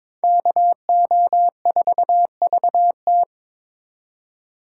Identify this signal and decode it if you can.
Morse code, heard as a keyed tone, about 27 wpm, 700 Hz. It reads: KO4VT
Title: Speed Racing - USA 2x2 Call Signs 22wpm